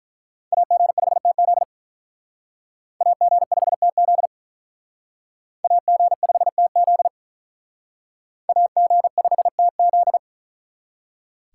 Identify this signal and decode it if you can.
Morse code, heard as a keyed tone, about 44 wpm, 700 Hz. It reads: AG5TZ AG5TZ AG5TZ AG5TZ